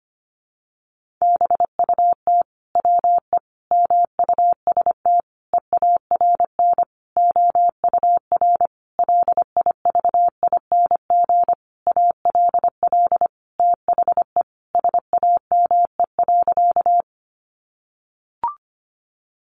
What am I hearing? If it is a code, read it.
Morse code, heard as a keyed tone, about 25 wpm, 700 Hz. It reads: BUT WE MUST EARN OUR LIVING ALL THE SAME.
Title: Sentences from Top 1500 Words 25wpm